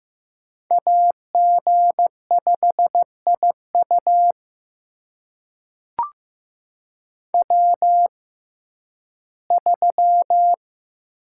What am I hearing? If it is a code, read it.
Morse code, heard as a keyed tone, about 15 wpm, spaced as for 11 wpm, 700 Hz. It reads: AG5IU W3
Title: USA Call Signs - 5 Characters - 6x Character Spacing - 15wpm